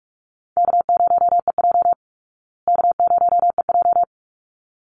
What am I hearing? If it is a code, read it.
Morse code, heard as a keyed tone, about 44 wpm, 700 Hz. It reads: K0EJ K0EJ
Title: Easy Call Signs Repeated 2x - 45wpm